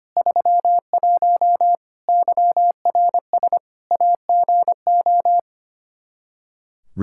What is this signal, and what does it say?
Morse code, heard as a keyed tone, about 25 wpm, 700 Hz. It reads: 31 YRS AGO
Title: Practice QSO - Exchange 3 - More Info - 25wpm